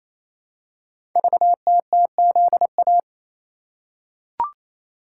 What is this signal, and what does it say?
Morse code, heard as a keyed tone, about 28 wpm, 700 Hz. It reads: VTTZA
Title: Code Groups - 5 Letters - 28wpm